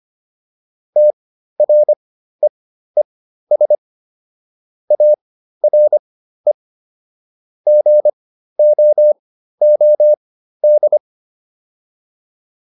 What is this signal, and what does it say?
Morse code, heard as a keyed tone, about 25 wpm, spaced as for 13 wpm, 600 Hz. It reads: TREES ARE GOOD
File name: PR405 13